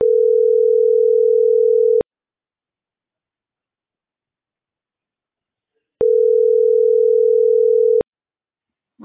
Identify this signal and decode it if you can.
Ringback tone (North America), 440 + 480 Hz.